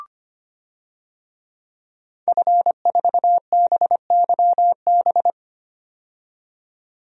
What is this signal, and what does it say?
Morse code, heard as a keyed tone, about 25 wpm, 700 Hz. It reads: F4BYB
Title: DXCC-100 Call Signs - 5 Characters - 25wpm